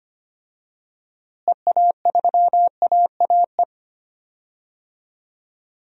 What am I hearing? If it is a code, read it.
Morse code, heard as a keyed tone, about 25 wpm, 700 Hz. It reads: EA3AAE
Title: DXCC-100 Call Signs - 6 Characters - Repeated 3x - 25wpm